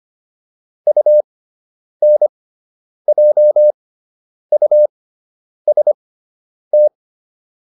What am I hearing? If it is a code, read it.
Morse code, heard as a keyed tone, about 25 wpm, spaced as for 9 wpm, 600 Hz. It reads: UNJUST